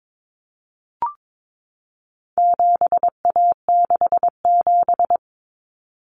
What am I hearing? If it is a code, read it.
Morse code, heard as a keyed tone, about 22 wpm, 700 Hz. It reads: 7A67
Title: Common Cut Numbers - 4-Digit 22wpm